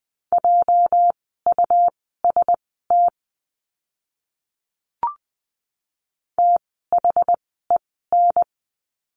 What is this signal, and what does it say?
Morse code, heard as a keyed tone, about 20 wpm, spaced as for 14 wpm, 700 Hz. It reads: JUST THEN
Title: 237 Lesson 38f - Letter q - Words 20wpm